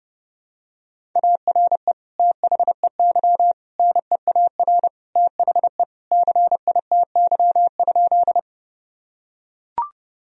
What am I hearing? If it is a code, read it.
Morse code, heard as a keyed tone, about 32 wpm, 700 Hz. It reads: ARE THEY NEAR THE CITY?
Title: Sentences from Top 1000 Words - Review Part 1 - 30wpm